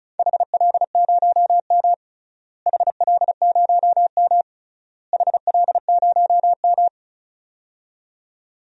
Morse code, heard as a keyed tone, about 34 wpm, 700 Hz. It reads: HL0M HL0M HL0M
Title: DXCC-100 Call Signs - 4 Characters - Repeated 3x - 35wpm